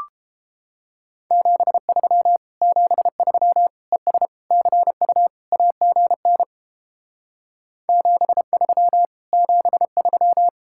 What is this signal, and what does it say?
Morse code, heard as a keyed tone, about 29 wpm, 700 Hz. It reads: 73 73 ES CU AGN 73 73